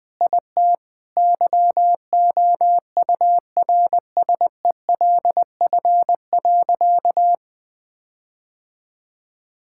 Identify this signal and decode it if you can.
Morse code, heard as a keyed tone, about 21 wpm, 700 Hz. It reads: IT YOURSELF.
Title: Sentences with 7-Word Phrases - Spoken Beginning - 20wpm